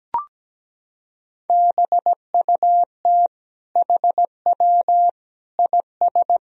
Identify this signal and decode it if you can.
Morse code, heard as a keyed tone, about 17 wpm, 700 Hz. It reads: BUT HW IS
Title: Sentences from Top 100 Words and Abbreviations 17wpm